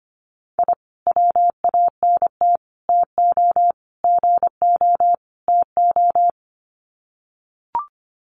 Morse code, heard as a keyed tone, about 25 wpm, 690 Hz. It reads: I WANT TO GO TO
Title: Sets of 5 Words Encore 25wpm